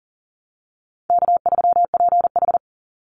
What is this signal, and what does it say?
Morse code, heard as a keyed tone, about 38 wpm, 700 Hz. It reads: K3PH
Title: Easy Call Signs - 40wpm